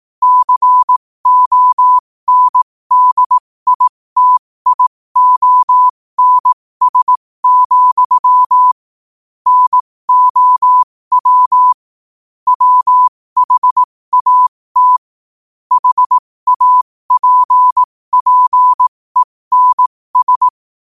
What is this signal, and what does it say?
Morse code, heard as a keyed tone, about 18 wpm, spaced as for 15 wpm, 1000 Hz. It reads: CONDITIONS, NOW WHAT HAPPENS
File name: qotdfiles / 15wpm